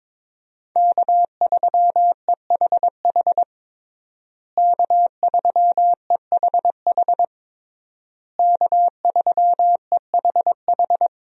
Morse code, heard as a keyed tone, about 21 wpm, 700 Hz. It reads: K3EHH K3EHH K3EHH
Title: USA Call Signs - 5 Characters - Repeated 3x - 22wpm